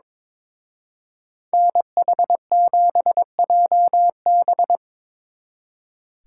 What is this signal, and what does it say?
Morse code, heard as a keyed tone, about 25 wpm, 700 Hz. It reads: NH7JB
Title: Speed Racing - USA 2x2 Call Signs 22wpm